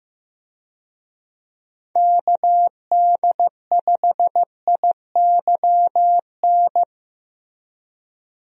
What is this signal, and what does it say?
Morse code, heard as a keyed tone, about 15 wpm, 700 Hz. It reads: KD5IYN
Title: USA Call Signs - 6 Characters - Repeated 3x - 15wpm